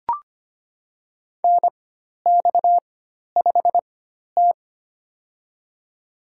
Morse code, heard as a keyed tone, about 25 wpm, spaced as for 12 wpm, 700 Hz. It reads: NX5T
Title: USA Call Signs - 4 Characters - 4x Character Spacing - 25wpm